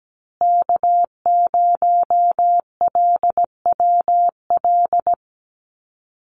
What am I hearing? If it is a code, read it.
Morse code, heard as a keyed tone, about 17 wpm, 700 Hz. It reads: K0LWL
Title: USA Call Signs Repeated 3x 17wpm